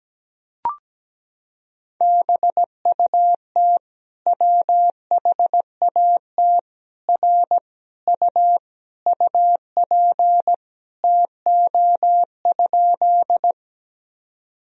Morse code, heard as a keyed tone, about 17 wpm, 700 Hz. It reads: BUT WHAT R U UP TO?
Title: Sentences from Top 100 Words and Abbreviations 17wpm